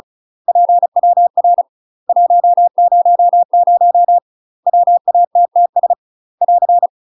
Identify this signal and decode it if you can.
Morse code, heard as a keyed tone, about 35 wpm, 700 Hz. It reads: PWR 100 WATTS <AR>
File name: UK MASTER QSOs TRAINING / 35WPM / UK MASTER QSOs 0003